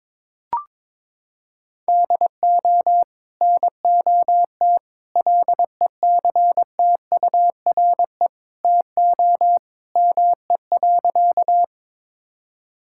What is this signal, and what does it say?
Morse code, heard as a keyed tone, about 21 wpm, 700 Hz. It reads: DO NOT LECTURE TO ME.
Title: Sentences from Top 1600 Words 22wpm